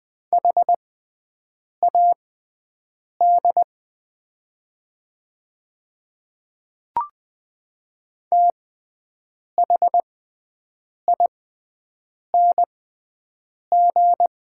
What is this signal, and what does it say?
Morse code, heard as a keyed tone, about 20 wpm, spaced as for 7 wpm, 700 Hz. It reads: HAD THING
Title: 6x Character Spacing - Top 100 Words - 20wpm